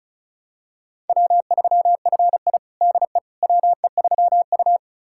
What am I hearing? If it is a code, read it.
Morse code, heard as a keyed tone, about 34 wpm, 700 Hz. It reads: W3FI DE WE3U
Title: USA Callsign DE Callsign - 4 Characters - 35wpm